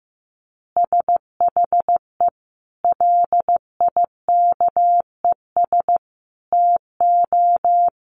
Morse code, heard as a keyed tone, about 15 wpm, 700 Hz. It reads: SHE LIKES TO
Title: Additional Easy Phrases 15wpm